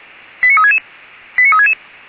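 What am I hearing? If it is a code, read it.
Five-tone selective call (ZVEI), heard as quick runs of five tones, tones of about 70 ms. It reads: 89366 89366
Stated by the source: ZVEI five-tone selective call received by SDR, logged by Feuerwehr Orlamünde (fire brigade).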